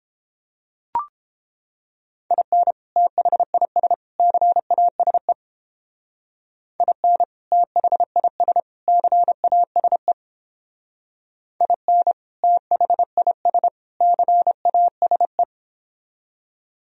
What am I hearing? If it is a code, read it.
Morse code, heard as a keyed tone, about 29 wpm, 700 Hz. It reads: IN THIS CASE IN THIS CASE IN THIS CASE